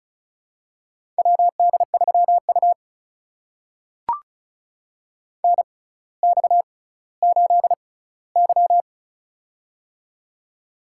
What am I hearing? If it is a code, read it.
Morse code, heard as a keyed tone, about 36 wpm, spaced as for 18 wpm, 700 Hz. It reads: WD3U NX8Y
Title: USA Call Signs - 4 Characters - 6x Character Spacing - 35wpm